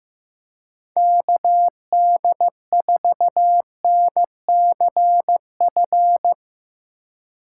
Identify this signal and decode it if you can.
Morse code, heard as a keyed tone, about 16 wpm, 700 Hz. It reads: KD4NCF
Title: Speed Racing - USA 2x3 Call Signs 15wpm